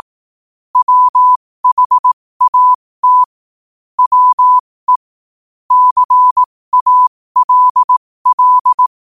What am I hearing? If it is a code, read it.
Morse code, heard as a keyed tone, about 18 wpm, spaced as for 15 wpm, 1000 Hz. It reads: WHAT WE CALL
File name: qotdfiles / 15wpm